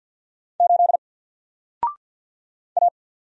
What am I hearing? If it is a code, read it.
Morse code, heard as a keyed tone, about 50 wpm, 700 Hz. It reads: 8 A
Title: Single Character (Letter:Number:Punctuation:Prosign) 50wpm